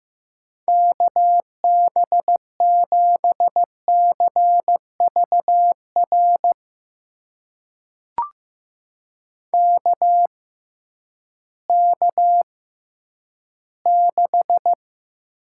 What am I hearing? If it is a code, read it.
Morse code, heard as a keyed tone, about 15 wpm, spaced as for 10 wpm, 700 Hz. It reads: KB7CVR KK6
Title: USA Call Signs - 6 Characters - 6x Character Spacing - 15wpm